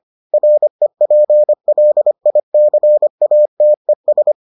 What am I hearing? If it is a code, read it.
Morse code, heard as a keyed tone, about 25 wpm, 600 Hz. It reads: REPLICATES